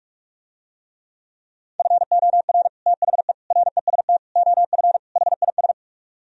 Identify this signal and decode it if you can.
Morse code, heard as a keyed tone, about 49 wpm, 700 Hz. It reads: FOR THE REST OF HIS